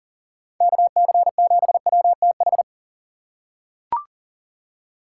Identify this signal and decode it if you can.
Morse code, heard as a keyed tone, about 40 wpm, 700 Hz. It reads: KC7WTH
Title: USA Call Signs - 6 Characters - 2x Character Spacing - 40wpm